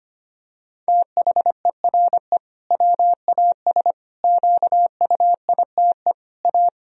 Morse code, heard as a keyed tone, about 25 wpm, 700 Hz. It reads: THERE WAS QUITE A
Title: Sentences from Top 1000 Words - Review Part 1 - 25wpm